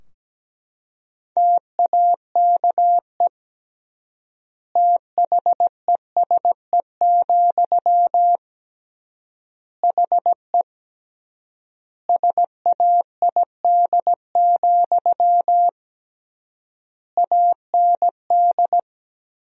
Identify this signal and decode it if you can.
Morse code, heard as a keyed tone, about 17 wpm, 700 Hz. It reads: TAKE THESE, HE SAID, AND
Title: 3x Word-Spacing - Sentences from Top 300 Words - 17wpm